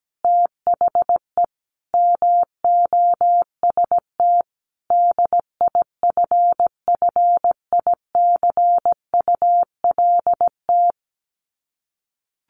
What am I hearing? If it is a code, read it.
Morse code, heard as a keyed tone, about 17 wpm, 700 Hz. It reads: THE MOST DIFFICULT
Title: Sets of 5 Words Encore2 17wpm